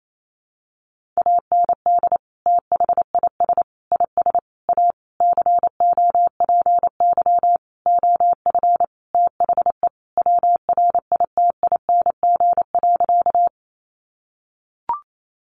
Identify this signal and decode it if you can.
Morse code, heard as a keyed tone, about 28 wpm, 700 Hz. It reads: AND THIS IS A COPY OF THE WRITING.
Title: Sentences from Top 1400 Words 28wpm